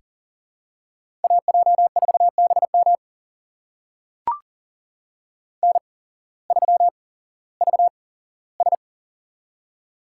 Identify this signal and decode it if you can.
Morse code, heard as a keyed tone, about 40 wpm, spaced as for 19 wpm, 700 Hz. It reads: AJ4BM N3VS
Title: USA Call Signs - 8x Character Spacing - 40wpm